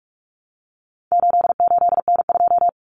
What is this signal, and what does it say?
Morse code, heard as a keyed tone, about 44 wpm, 700 Hz. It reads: 88N2
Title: Common Cut Numbers - 4-Digit 45wpm